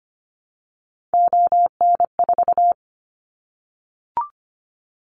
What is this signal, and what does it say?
Morse code, heard as a keyed tone, about 25 wpm, 700 Hz. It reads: ON4
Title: Callsign Prefixes for Top 25 DXCC - 25wpm